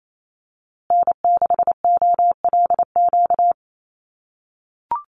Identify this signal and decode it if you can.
Morse code, heard as a keyed tone, about 28 wpm, 700 Hz. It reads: N6OLQ
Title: USA Call Signs Repeated 3x 28wpm